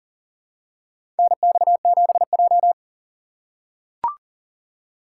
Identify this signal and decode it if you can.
Morse code, heard as a keyed tone, about 40 wpm, 700 Hz. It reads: NX7J
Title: USA Call Signs - 4 Characters - 4x Character Spacing - 40wpm